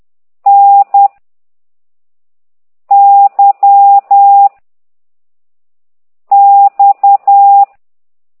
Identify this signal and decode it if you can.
Morse code, heard as a keyed tone, about 10 wpm, 800 Hz. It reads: N Y X
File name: ct2idw / exerc9